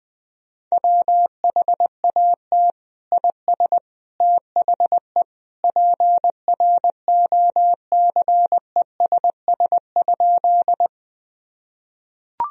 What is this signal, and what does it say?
Morse code, heard as a keyed tone, about 20 wpm, 700 Hz. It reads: WHAT IS THE PROCESS?
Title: Sentences from Top 1000 Words - Review Part 3 - 20wpm